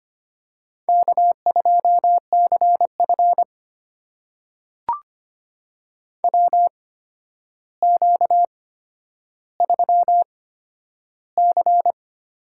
Morse code, heard as a keyed tone, about 25 wpm, spaced as for 11 wpm, 700 Hz. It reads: K2CF WQ3C
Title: USA Call Signs - 4 Characters - 8x Character Spacing - 25wpm